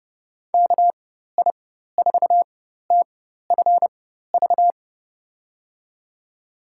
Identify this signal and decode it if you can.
Morse code, heard as a keyed tone, about 32 wpm, spaced as for 14 wpm, 700 Hz. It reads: KI4TFV
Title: USA Call Signs - 6 Characters - 4x Character Spacing - 30wpm